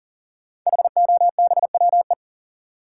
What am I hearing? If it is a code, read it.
Morse code, heard as a keyed tone, about 40 wpm, 690 Hz. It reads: HOBWE
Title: Code Groups - 5 Letters - 40wpm